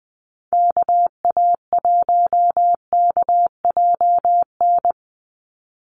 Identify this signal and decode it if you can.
Morse code, heard as a keyed tone, about 20 wpm, 700 Hz. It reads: KA1KJN
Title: USA Call Signs 20wpm